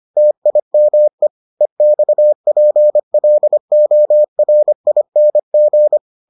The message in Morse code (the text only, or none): TIME EXPLORING